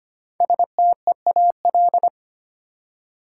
Morse code, heard as a keyed tone, about 25 wpm, 700 Hz. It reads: STEAL
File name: 5 Letter Words - First 3 Letters Given - 25wpm